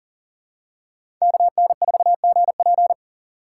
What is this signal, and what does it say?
Morse code, heard as a keyed tone, about 40 wpm, 700 Hz. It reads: KN4GP
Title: USA Call Signs - 3x Character Spacing - 40wpm